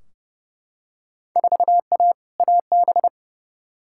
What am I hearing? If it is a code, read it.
Morse code, heard as a keyed tone, about 32 wpm, 700 Hz. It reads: 4A AB
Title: Field Day Exchanges Encore - 30wpm